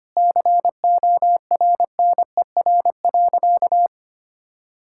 Morse code, heard as a keyed tone, about 25 wpm, 700 Hz. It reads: CORNER.
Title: Sentences from Top 1600 Words 25wpm